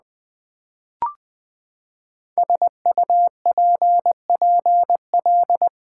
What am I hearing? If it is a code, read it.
Morse code, heard as a keyed tone, about 20 wpm, 700 Hz. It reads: SUPPL
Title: Build A Word - 5 Letters At A Time - 10-Letter Words - 20wpm